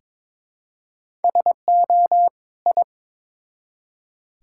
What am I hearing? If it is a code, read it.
Morse code, heard as a keyed tone, about 22 wpm, 700 Hz. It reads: SO I
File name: Build A Phrase - 2-Word Phrases - 22wpm